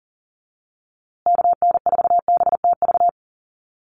Morse code, heard as a keyed tone, about 38 wpm, 700 Hz. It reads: KN4BTV
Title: USA Call Signs Repeated 2x 40wpm